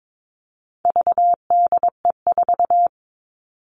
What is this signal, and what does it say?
Morse code, heard as a keyed tone, about 22 wpm, 700 Hz. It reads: VDE4
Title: All Cut Numbers - 4-Digit 22wpm